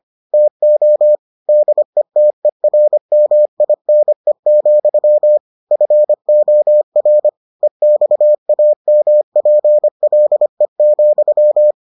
Morse code, heard as a keyed tone, about 25 wpm, 600 Hz. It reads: TO DETERMINE, FOR EXAMPLE,